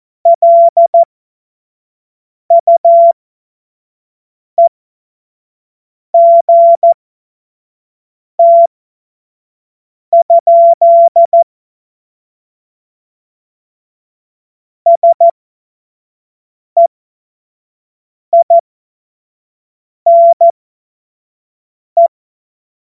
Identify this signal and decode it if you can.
Morse code, heard as a keyed tone, about 14 wpm, spaced as for 5 wpm, 680 Hz. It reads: LUEGT? SEINE